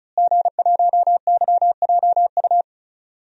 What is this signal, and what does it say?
Morse code, heard as a keyed tone, about 35 wpm, 700 Hz. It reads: G1YJU
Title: DXCC-100 Call Signs - 5 Characters - Repeated 3x - 35wpm